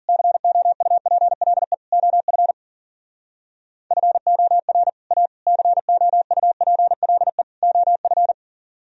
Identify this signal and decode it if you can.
Morse code, heard as a keyed tone, about 52 wpm, 700 Hz. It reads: COUPLE OF FOR A COUPLE OF